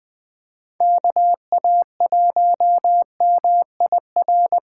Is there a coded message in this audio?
Yes: Morse code from 0.8 to 4.6 s, heard as a keyed tone, about 21 wpm, 700 Hz. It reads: KA1MIR